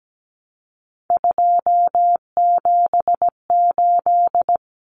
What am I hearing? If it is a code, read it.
Morse code, heard as a keyed tone, about 17 wpm, 700 Hz. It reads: 278
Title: Common Cut Numbers - 3-Digit 17wpm